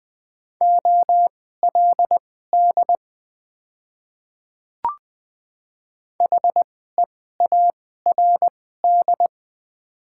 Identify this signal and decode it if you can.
Morse code, heard as a keyed tone, about 20 wpm, spaced as for 14 wpm, 700 Hz. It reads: OLD HEARD
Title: 2x Character Spacing - Top 500 Words - 20wpm